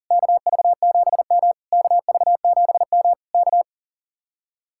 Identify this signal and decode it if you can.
Morse code, heard as a keyed tone, about 50 wpm, 700 Hz. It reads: KV7M KV7M K